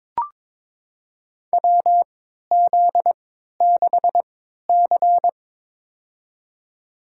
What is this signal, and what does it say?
Morse code, heard as a keyed tone, about 22 wpm, spaced as for 12 wpm, 700 Hz. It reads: WZ6C
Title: USA Call Signs - 3x Character Spacing - 22wpm